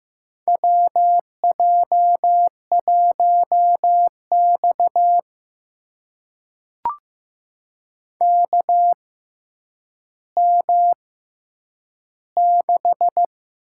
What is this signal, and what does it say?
Morse code, heard as a keyed tone, about 15 wpm, spaced as for 9 wpm, 700 Hz. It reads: WJ1X KM6
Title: USA Call Signs - 6x Character Spacing - 15wpm